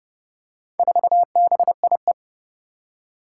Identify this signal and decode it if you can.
Morse code, heard as a keyed tone, about 29 wpm, 700 Hz. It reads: 4BIE
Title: Four Challenging Letter-Number Combinations 30wpm